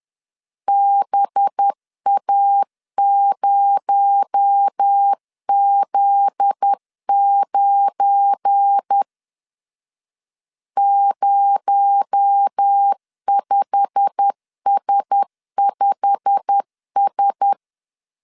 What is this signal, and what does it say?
Morse code, heard as a keyed tone, about 10 wpm, 800 Hz. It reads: BA0Z9 05S5S
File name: CW / DET / 1 9 1500C S B 10wpm 4439